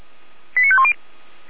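Five-tone selective call (ZVEI), heard as quick runs of five tones, tones of about 70 ms. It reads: 89510
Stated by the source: ZVEI five-tone selective call received by SDR.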